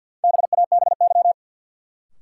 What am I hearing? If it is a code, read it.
Morse code, heard as a keyed tone, about 53 wpm, 700 Hz. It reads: BABY
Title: Sentences from Top 1300 Words - New Word Drill - 50wpm